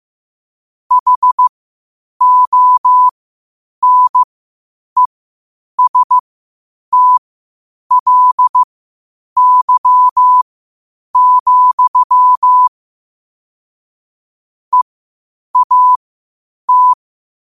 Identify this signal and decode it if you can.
Morse code, heard as a keyed tone, about 15 wpm, spaced as for 8 wpm, 1000 Hz. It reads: HONESTLY, EAT
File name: qotdfiles / 8wpm